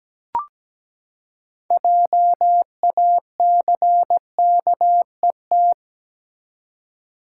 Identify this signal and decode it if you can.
Morse code, heard as a keyed tone, about 17 wpm, 700 Hz. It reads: JACKET
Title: Six Letter Words - 17wpm